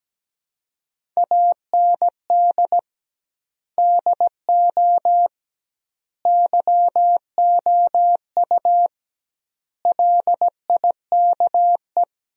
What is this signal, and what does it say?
Morse code, heard as a keyed tone, about 17 wpm, 700 Hz. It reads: AND DO YOU LIKE
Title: 2x Word-Spacing - Sentences from Top 100 Words - 17wpm